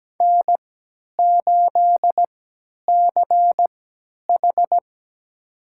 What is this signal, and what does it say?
Morse code, heard as a keyed tone, about 17 wpm, spaced as for 10 wpm, 700 Hz. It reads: N8CH
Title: USA Call Signs - 3x Character Spacing - 17wpm